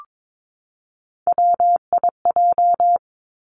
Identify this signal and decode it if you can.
Morse code, heard as a keyed tone, about 22 wpm, 700 Hz. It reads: WIJ
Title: Three Challenging Letter-Number Combinations 22wpm